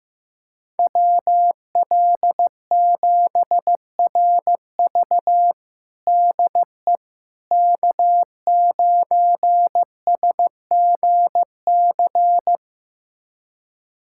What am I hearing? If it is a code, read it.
Morse code, heard as a keyed tone, about 15 wpm, 700 Hz. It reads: WL7RV DE K9SGC
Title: USA Callsign DE Callsign - 5 Characters - 15wpm